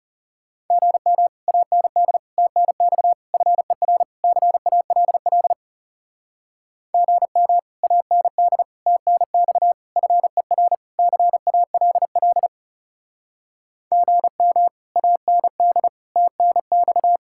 Morse code, heard as a keyed tone, about 37 wpm, 700 Hz. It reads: GM AND TNX FER CALL GM AND TNX FER CALL GM AND TNX
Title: Speed-Racing Common QSO Phrases - 30wpm